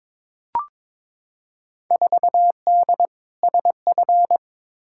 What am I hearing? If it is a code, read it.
Morse code, heard as a keyed tone, about 22 wpm, 700 Hz. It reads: 4D SF
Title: Field Day Exchanges Encore - 22wpm